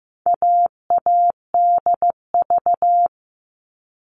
Morse code, heard as a keyed tone, about 15 wpm, 700 Hz. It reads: AADV
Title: All Cut Numbers - 4-Digit 15wpm